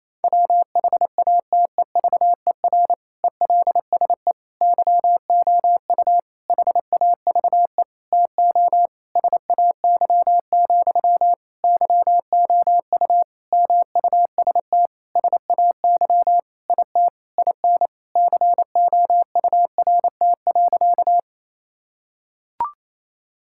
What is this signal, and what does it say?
Morse code, heard as a keyed tone, about 28 wpm, 700 Hz. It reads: WHATEVER ELSE YOU HAVE TO SAY, YOU MUST SAY IT IN COURT.
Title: Sentences from Top 1600 Words 28wpm